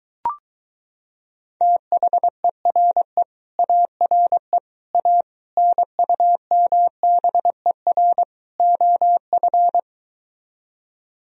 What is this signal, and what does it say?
Morse code, heard as a keyed tone, about 23 wpm, 700 Hz. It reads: THERE ARE A NUMBER OF